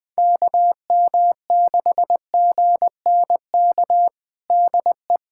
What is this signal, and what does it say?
Morse code, heard as a keyed tone, about 21 wpm, 700 Hz. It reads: KM6GNK DE
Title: USA Callsign DE Callsign - 6 Characters - 20wpm